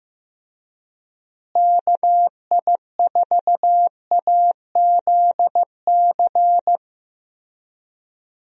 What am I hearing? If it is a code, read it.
Morse code, heard as a keyed tone, about 15 wpm, 700 Hz. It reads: KI4AZC
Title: USA Call Signs - 6 Characters - Repeated 3x - 15wpm